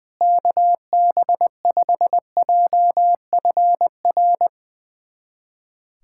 Morse code, heard as a keyed tone, about 21 wpm, 700 Hz. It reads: KB5JFR
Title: USA Callsign DE Callsign - 6 Characters - 20wpm